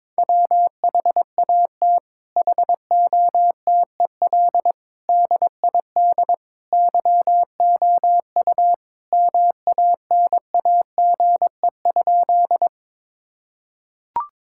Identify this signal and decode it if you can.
Morse code, heard as a keyed tone, about 22 wpm, 700 Hz. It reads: WHAT HOTEL DID YOU MANAGE?
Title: Sentences from Top 1600 Words 22wpm